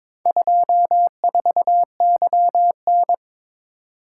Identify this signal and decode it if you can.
Morse code, heard as a keyed tone, about 22 wpm, 700 Hz. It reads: 24YN
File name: practice / 22wpm / Four Letter-Number Combinations - 22wpm